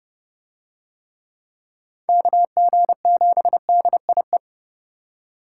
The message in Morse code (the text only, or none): KG7DIE